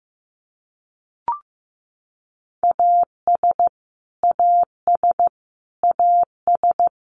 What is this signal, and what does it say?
Morse code, heard as a keyed tone, about 15 wpm, 700 Hz. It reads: AS AS AS
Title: Top 200 Words Repeated 15wpm